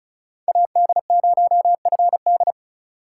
Morse code, heard as a keyed tone, about 34 wpm, 700 Hz. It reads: AD0FD
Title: USA Call Signs - 5 Characters - 2x Character Spacing - 35wpm